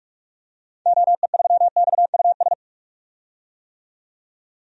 Morse code, heard as a keyed tone, about 46 wpm, 700 Hz. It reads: OE3XUS
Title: DXCC-100 Call Signs - 6 Characters - 45wpm